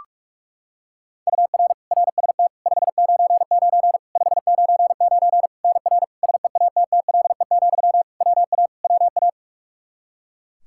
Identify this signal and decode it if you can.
Morse code, heard as a keyed tone, about 46 wpm, 700 Hz. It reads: UR RST 599 599 NR SEATTLE, WA WA
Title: Practice QSO - Exchange 1 - Beginning - 45wpm